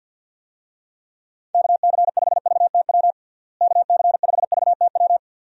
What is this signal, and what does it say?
Morse code, heard as a keyed tone, about 51 wpm, 700 Hz. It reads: KC5VTW KC5VTW
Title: USA Call Signs - 6 Characters - Repeated 2x - 50wpm